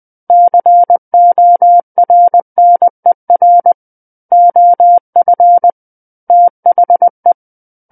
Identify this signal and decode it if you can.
Morse code, heard as a keyed tone, about 20 wpm, 700 Hz. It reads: CORNER OF THE
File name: 20 WPM